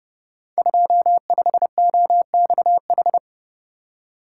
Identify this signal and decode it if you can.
Morse code, heard as a keyed tone, about 30 wpm, 700 Hz. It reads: 25OXH
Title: Code Groups - 5 Letter:Numbers - 30wpm